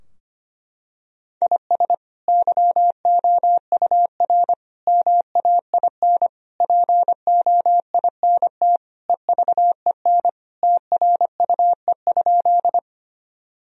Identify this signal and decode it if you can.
Morse code, heard as a keyed tone, about 25 wpm, 700 Hz. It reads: IS YOUR MAIN POINT EVEN TRUE?
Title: Sentences from Top 1000 Words - Review Part 2 - 25wpm